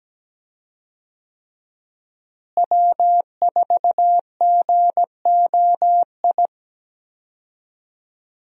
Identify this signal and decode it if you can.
Morse code, heard as a keyed tone, about 17 wpm, 700 Hz. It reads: W4GOI
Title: USA Call Signs - 5 Characters - Repeated 3x - 17wpm